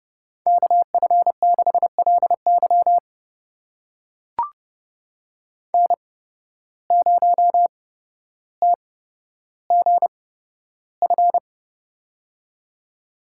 Morse code, heard as a keyed tone, about 32 wpm, spaced as for 13 wpm, 700 Hz. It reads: KF6LY N0TGF
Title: USA Call Signs - 5 Characters - 8x Character Spacing - 30wpm